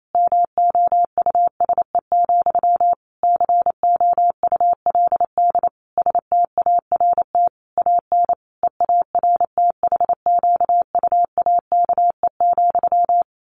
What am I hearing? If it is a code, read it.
Morse code, heard as a keyed tone, about 28 wpm, 700 Hz. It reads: MOUSE, COULD START AN EARTHQUAKE,
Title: A Sound Of Thunder 28wpm